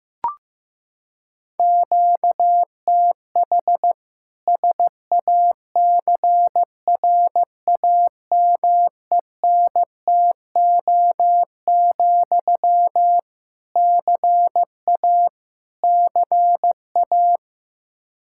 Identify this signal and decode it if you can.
Morse code, heard as a keyed tone, about 16 wpm, 700 Hz. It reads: QTH SACRAMENTO, CA CA
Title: K1USN SST QSO Practice 15wpm